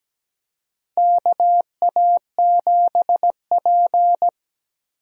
Morse code, heard as a keyed tone, about 17 wpm, 700 Hz. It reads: KA7P
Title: USA Call Signs - 4 Characters - Repeated 3x - 17wpm